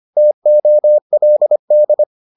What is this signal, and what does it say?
Morse code, heard as a keyed tone, about 25 wpm, 600 Hz. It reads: TOLD